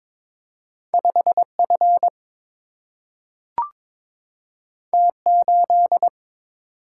Morse code, heard as a keyed tone, about 22 wpm, 700 Hz. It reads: 5 F T 8
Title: Single-Letter Single-Number 22wpm